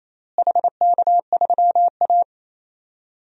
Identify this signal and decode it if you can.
Morse code, heard as a keyed tone, about 28 wpm, 700 Hz. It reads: HK3A
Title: DXCC-100 Call Signs - 4 Characters - Repeated 3x - 28wpm